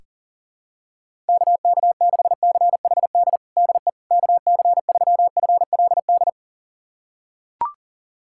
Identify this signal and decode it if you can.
Morse code, heard as a keyed tone, about 40 wpm, 700 Hz. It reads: KK6CHD DE KC3FLD
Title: USA Callsign DE Callsign - 6 Characters - 40wpm